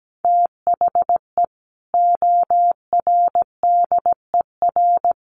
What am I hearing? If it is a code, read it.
Morse code, heard as a keyed tone, about 17 wpm, 700 Hz. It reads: THE ORDER
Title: Sentences from Top 1500 Words 17wpm